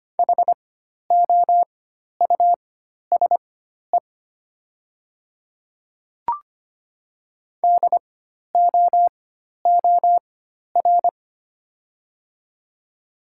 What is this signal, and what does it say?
Morse code, heard as a keyed tone, about 25 wpm, spaced as for 12 wpm, 700 Hz. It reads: HOUSE DOOR
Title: 4x Character Spacing - Top 500 Words - 25wpm